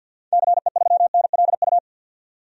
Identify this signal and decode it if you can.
Morse code, heard as a keyed tone, about 46 wpm, 700 Hz. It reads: CE3NLU
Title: DXCC-100 Call Signs - 6 Characters - 50wpm